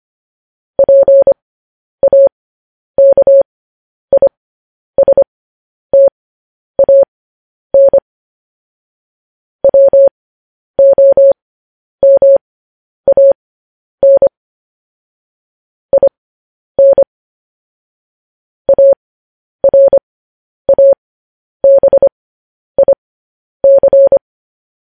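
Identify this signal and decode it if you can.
Morse code, heard as a keyed tone, about 25 wpm, spaced as for 10 wpm, 550 Hz. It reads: PAKISTAN WOMAN IN ARABIC
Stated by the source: Morse code news headlines practice, at 10 WPM.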